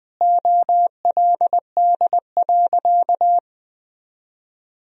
Morse code, heard as a keyed tone, about 21 wpm, 700 Hz. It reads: OLD.
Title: Sentences from Top 1000 Words - Review Part 1 - 20wpm